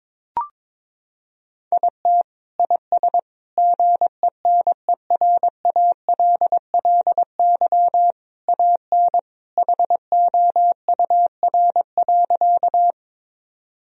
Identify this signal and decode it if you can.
Morse code, heard as a keyed tone, about 22 wpm, 700 Hz. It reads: IT IS GENERALLY AN HOUR.
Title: Sentences from Top 1000 Words - Review Part 4 - 22wpm